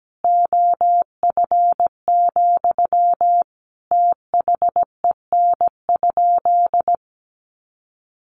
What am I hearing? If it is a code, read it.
Morse code, heard as a keyed tone, about 17 wpm, 700 Hz. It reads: OF, THEN?
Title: Sentences from Top 1500 Words 17wpm